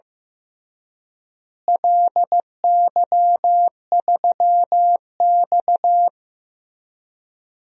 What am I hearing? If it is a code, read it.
Morse code, heard as a keyed tone, about 15 wpm, 700 Hz. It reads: LY3X
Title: DXCC-100 Call Signs - 4 Characters - Repeated 3x - 15wpm